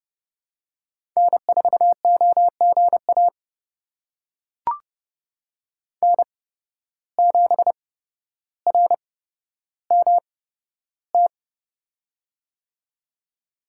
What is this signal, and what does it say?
Morse code, heard as a keyed tone, about 30 wpm, spaced as for 13 wpm, 700 Hz. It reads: N4OGA N7RMT
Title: USA Call Signs - 5 Characters - 8x Character Spacing - 30wpm